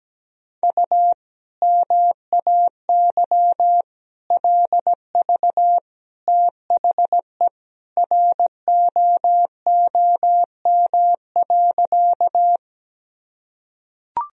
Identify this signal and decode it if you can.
Morse code, heard as a keyed tone, about 17 wpm, 700 Hz. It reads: U MAY LV THE ROOM.
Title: Sentences from Top 400 Words and Abbreviations 17wpm